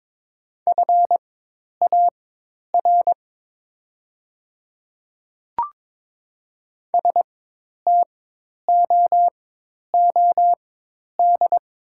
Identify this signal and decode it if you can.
Morse code, heard as a keyed tone, about 22 wpm, spaced as for 10 wpm, 700 Hz. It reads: FAR STOOD
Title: 4x Character Spacing - Top 500 Words - 22wpm